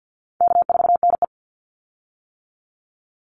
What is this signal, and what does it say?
Morse code, heard as a keyed tone, about 50 wpm, 700 Hz. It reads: K4NE
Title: Easy Call Signs - 50wpm